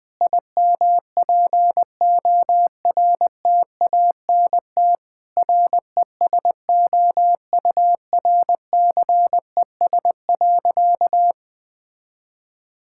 Morse code, heard as a keyed tone, about 20 wpm, 700 Hz. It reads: IMPORTANT RESOURCES.
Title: Sentences with 5-Word Phrases - Spoken Beginning - 20wpm